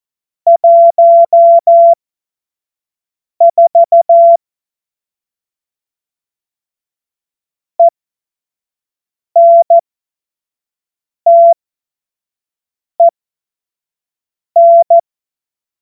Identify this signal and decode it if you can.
Morse code, heard as a keyed tone, about 14 wpm, spaced as for 5 wpm, 680 Hz. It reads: 14 ENTEN
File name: Morsekurs / Witze / Witz 13. April 2021, 25 F70